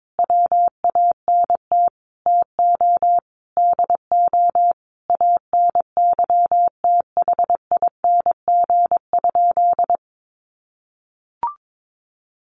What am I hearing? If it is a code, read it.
Morse code, heard as a keyed tone, about 22 wpm, 700 Hz. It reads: WANT TO DO ANYTHING?
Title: Sentences from Top 1000 Words 22wpm